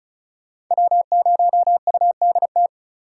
Morse code, heard as a keyed tone, about 47 wpm, spaced as for 41 wpm, 700 Hz. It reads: W0UDT
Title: Speed Racing - USA 1x3 Call Signs 35wpm